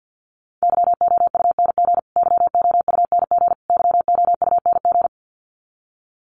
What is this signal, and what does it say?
Morse code, heard as a keyed tone, about 49 wpm, 700 Hz. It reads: YOUNG YOUNG YOUNG
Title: Top 500 Words Repeated 50wpm